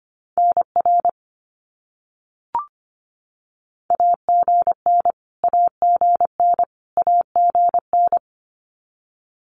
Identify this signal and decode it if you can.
Morse code, heard as a keyed tone, about 25 wpm, 700 Hz. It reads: NR AGN AGN AGN
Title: Abbreviations Repeated 3x - 25wpm